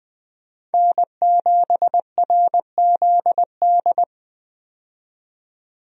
Morse code, heard as a keyed tone, about 21 wpm, 700 Hz. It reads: N7RZD
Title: USA Call Signs - 5 Characters - 20wpm